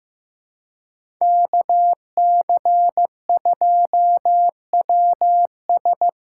Morse code, heard as a keyed tone, about 15 wpm, 700 Hz. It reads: KC2WS
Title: USA Call Signs - 6 Characters - 6x Character Spacing - 15wpm